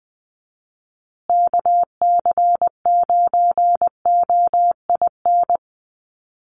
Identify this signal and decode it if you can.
Morse code, heard as a keyed tone, about 20 wpm, 700 Hz. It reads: KC9OIN
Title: USA Call Signs Repeated 2x 20wpm